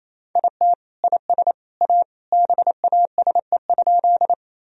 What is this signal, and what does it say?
Morse code, heard as a keyed tone, about 28 wpm, 700 Hz. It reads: IT IS A BASE?
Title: Sentences from Top 1000 Words - Review Part 1 - 28wpm